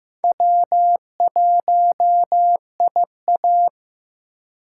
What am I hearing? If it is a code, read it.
Morse code, heard as a keyed tone, about 15 wpm, 700 Hz. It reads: W1IA
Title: USA Call Signs - 4 Characters - 4x Character Spacing - 15wpm